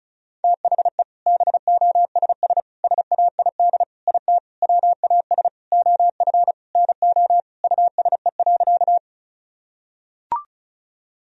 Morse code, heard as a keyed tone, about 34 wpm, 700 Hz. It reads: THE BOSS SAID IT WAS OF NO USE.
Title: Sentences from Top 1000 Words - Review Part 4 - 35wpm